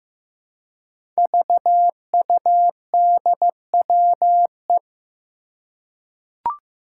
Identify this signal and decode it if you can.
Morse code, heard as a keyed tone, about 15 wpm, 700 Hz. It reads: VUDWE
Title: Five Challenging Letter-Number Combinations 15wpm